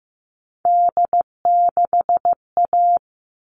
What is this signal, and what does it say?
Morse code, heard as a keyed tone, about 15 wpm, 700 Hz. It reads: D6A
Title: All Cut Numbers - 3-Digit 15wpm